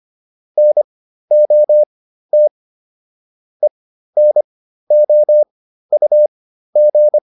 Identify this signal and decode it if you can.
Morse code, heard as a keyed tone, about 25 wpm, spaced as for 13 wpm, 600 Hz. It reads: NOT ENOUG